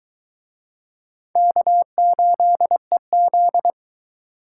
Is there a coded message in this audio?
Yes: Morse code from 1.4 to 3.7 s, heard as a keyed tone, about 23 wpm, 700 Hz. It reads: K8EZ